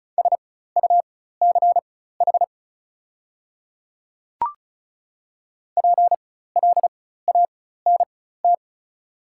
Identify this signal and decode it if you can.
Morse code, heard as a keyed tone, about 36 wpm, spaced as for 16 wpm, 690 Hz. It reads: SUCH PLANT
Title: 4x Character Spacing - Top 500 Words - 35wpm